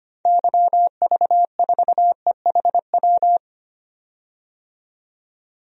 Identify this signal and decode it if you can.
Morse code, heard as a keyed tone, about 25 wpm, 700 Hz. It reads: YV4EHW
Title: DXCC-100 Call Signs - 6 Characters - 25wpm